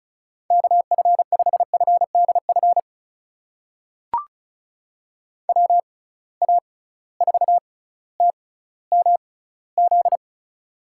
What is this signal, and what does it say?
Morse code, heard as a keyed tone, about 36 wpm, spaced as for 18 wpm, 700 Hz. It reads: KF5FDF WA4TMZ
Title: USA Call Signs - 6 Characters - 6x Character Spacing - 35wpm